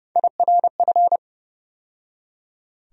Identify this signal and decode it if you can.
Morse code, heard as a keyed tone, about 30 wpm, 700 Hz. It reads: IRF
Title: USA Call Sign Postfixes - 3 Letters 30wpm